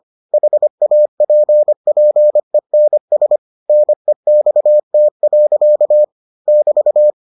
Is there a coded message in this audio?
Yes: Morse code from 0.3 to 7.1 s, heard as a keyed tone, about 25 wpm, 600 Hz. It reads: HAPPENS NEXT. <BT>